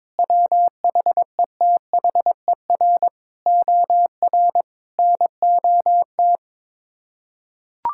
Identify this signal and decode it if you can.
Morse code, heard as a keyed tone, about 22 wpm, 700 Hz. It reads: WHETHER OR NOT
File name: practice / 22wpm / Sets of 8 Words - 22wpm